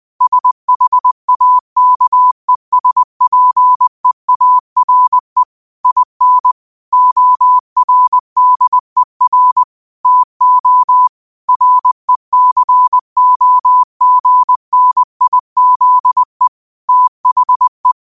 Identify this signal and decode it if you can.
Morse code, heard as a keyed tone, about 20 wpm, 1000 Hz. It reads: SHAKESPEARE IN ORDER TO RECOGNIZE THE
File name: qotdfiles / 20wpm